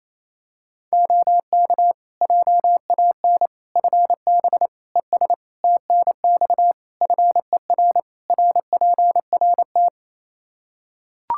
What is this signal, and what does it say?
Morse code, heard as a keyed tone, about 28 wpm, 700 Hz. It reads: OK JAN FB ES TNX FER RPRT
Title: K1USN SST QSO Practice 28wpm